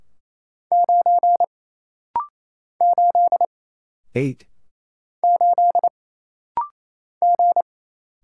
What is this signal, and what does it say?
Morse code, heard as a keyed tone, about 28 wpm, 700 Hz. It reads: 9 8 8 G
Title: Single Letter-Number Alt - Rapid-Fire 28wpm